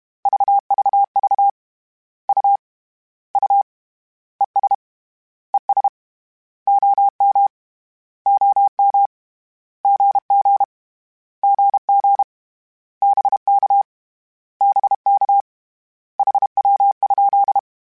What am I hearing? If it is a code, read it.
Morse code, heard as a keyed tone, about 32 wpm, 800 Hz. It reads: VVV U U ES ES OM OM GG GG BK BK HW?
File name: Abreviations / Abreviations mp3s / Abreviations@30